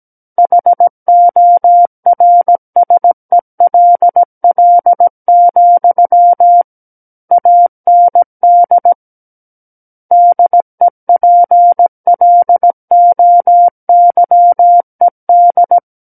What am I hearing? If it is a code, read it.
Morse code, heard as a keyed tone, about 17 wpm, 700 Hz. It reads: HORSELL, AND DEPLOYED
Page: SKCC Learning Center - Morse Code Practice Files